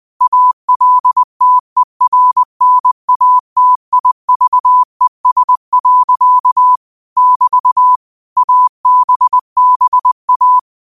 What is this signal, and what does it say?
Morse code, heard as a keyed tone, about 20 wpm, 1000 Hz. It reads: ALTERNATIVES. <BT> ABBA
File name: qotdfiles / 20wpm